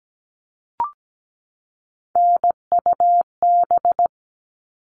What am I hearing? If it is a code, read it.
Morse code, heard as a keyed tone, about 17 wpm, 700 Hz. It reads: NUB
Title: All Cut Numbers - 3-Digit 17wpm